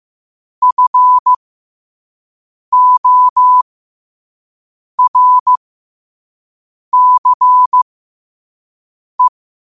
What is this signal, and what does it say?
Morse code, heard as a keyed tone, about 15 wpm, spaced as for 5 wpm, 1000 Hz. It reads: FORCE
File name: 5wpm